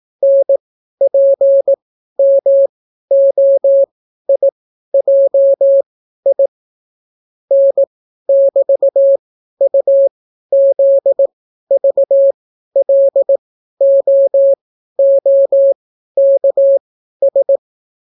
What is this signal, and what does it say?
Morse code, heard as a keyed tone, about 18 wpm, spaced as for 12 wpm, 550 Hz. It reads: NPMOIJI N<BT>UZVLOOKS